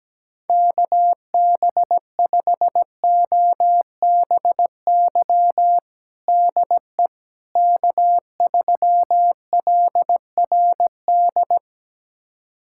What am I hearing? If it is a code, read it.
Morse code, heard as a keyed tone, about 17 wpm, 700 Hz. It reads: KB5OBY DE K3LRD
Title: USA Callsign DE Callsign 17wpm